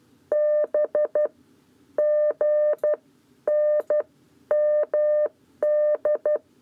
Morse code, heard as a keyed tone, about 12 wpm, 600 Hz. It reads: B G N MD
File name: 7b lezione cw 1